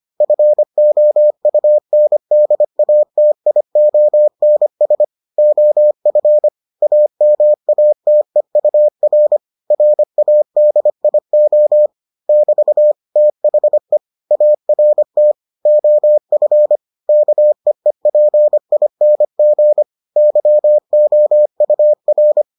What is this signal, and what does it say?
Morse code, heard as a keyed tone, about 25 wpm, 600 Hz. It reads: FOUNDATIONS OF AMATEUR RADIO <BT> THE ART OF KEEPING YOUR